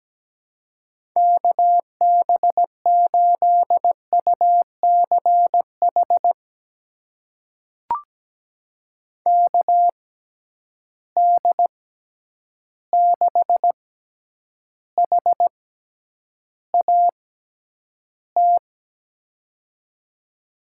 Morse code, heard as a keyed tone, about 17 wpm, spaced as for 9 wpm, 700 Hz. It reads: KB8UCH KD6HAT
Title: USA Call Signs - 6 Characters - 6x Character Spacing - 17wpm